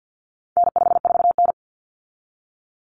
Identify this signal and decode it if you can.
Morse code, heard as a keyed tone, about 48 wpm, 700 Hz. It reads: N54N
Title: Common Cut Numbers - 4-Digit 50wpm